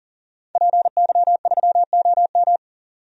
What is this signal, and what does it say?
Morse code, heard as a keyed tone, about 40 wpm, 700 Hz. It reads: PY3OM
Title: DXCC-100 Call Signs - 5 Characters - 40wpm